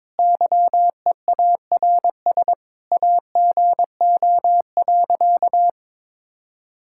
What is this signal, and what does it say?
Morse code, heard as a keyed tone, about 22 wpm, 700 Hz. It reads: YEARS AGO.
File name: practice / 22wpm / Sentences with 4-Word Phrases - 22wpm